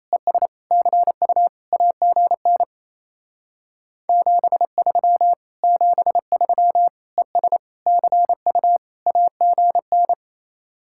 Morse code, heard as a keyed tone, about 36 wpm, 700 Hz. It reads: ES CU AGN 73 73 ES CU AGN